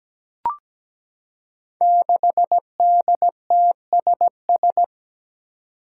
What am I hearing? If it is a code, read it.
Morse code, heard as a keyed tone, about 17 wpm, 700 Hz. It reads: 6DTSS
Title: Five Challenging Letter-Number Combinations 17wpm